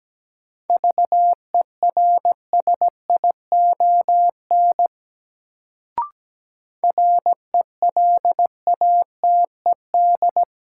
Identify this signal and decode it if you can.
Morse code, heard as a keyed tone, about 17 wpm, 700 Hz. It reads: VERSION RELATED
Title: Sentences from Top 1400 Words - New Word Drill Alt - 17wpm